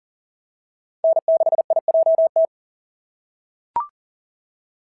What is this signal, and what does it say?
Morse code, heard as a keyed tone, about 40 wpm, 660 Hz. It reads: N6IJT
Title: SOTA QSO Practice 40wpm